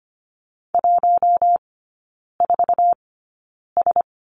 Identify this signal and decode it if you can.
Morse code, heard as a keyed tone, about 25 wpm, spaced as for 9 wpm, 700 Hz. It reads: 14S
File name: Lesson 15e - Number 5 - Three Characters 25wpm